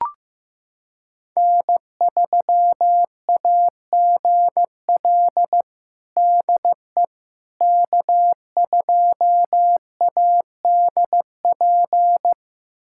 Morse code, heard as a keyed tone, about 15 wpm, 700 Hz. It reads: N3AGL DE K2ADP
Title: USA Callsign DE Callsign - 5 Characters - 15wpm